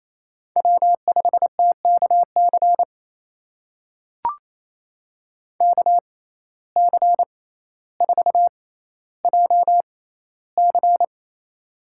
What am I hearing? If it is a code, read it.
Morse code, heard as a keyed tone, about 28 wpm, spaced as for 14 wpm, 700 Hz. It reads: W5TKC KC4JC
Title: USA Call Signs - 5 Characters - 6x Character Spacing - 28wpm